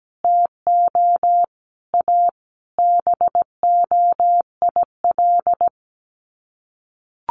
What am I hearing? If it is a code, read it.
Morse code, heard as a keyed tone, about 17 wpm, 700 Hz. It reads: TO A BOIL
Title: Sets of 5 Words Encore 17wpm